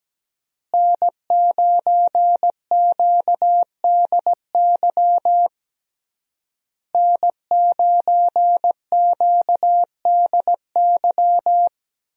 Morse code, heard as a keyed tone, about 17 wpm, 700 Hz. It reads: N9QDY N9QDY
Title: USA Call Signs - 5 Characters - Repeated 3x - 17wpm